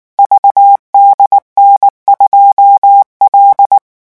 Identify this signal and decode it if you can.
Morse code, heard as a keyed tone, about 19 wpm, 800 Hz. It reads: VDN2L
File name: morse mp3 / revision10-vitesse 18 mots